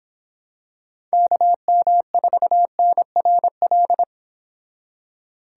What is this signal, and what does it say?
Morse code, heard as a keyed tone, about 26 wpm, 700 Hz. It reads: KM4NRL